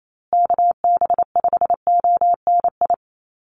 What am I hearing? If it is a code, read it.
Morse code, heard as a keyed tone, about 28 wpm, 700 Hz. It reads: KB5ONI
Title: USA Call Signs Repeated 3x 28wpm